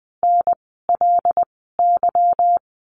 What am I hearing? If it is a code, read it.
Morse code, heard as a keyed tone, about 20 wpm, spaced as for 15 wpm, 700 Hz. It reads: NLY
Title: 146 Lesson 24f - Letter y - Words 20wpm